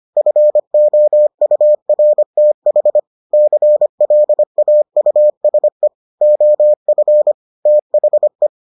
Morse code, heard as a keyed tone, about 25 wpm, 600 Hz. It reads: FOURTH CLAUSE OF THE